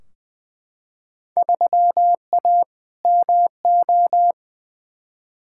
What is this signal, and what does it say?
Morse code, heard as a keyed tone, about 20 wpm, 700 Hz. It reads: 3A MO
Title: Field Day Exchanges Encore - 20wpm